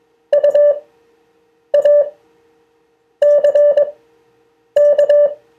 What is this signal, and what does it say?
Morse code, heard as a keyed tone, about 21 wpm, spaced as for 7 wpm, 580 Hz. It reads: UACK
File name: 2of4-words-1